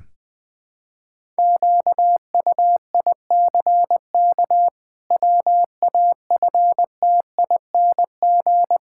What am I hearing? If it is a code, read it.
Morse code, heard as a keyed tone, about 21 wpm, 700 Hz. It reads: QUICK WAFTING
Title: Pangram Challenge v2 - 20wpm